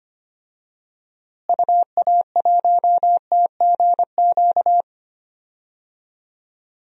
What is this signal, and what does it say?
Morse code, heard as a keyed tone, about 25 wpm, 700 Hz. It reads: UA1TGQ
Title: DXCC-100 Call Signs - 6 Characters - Repeated 2x - 25wpm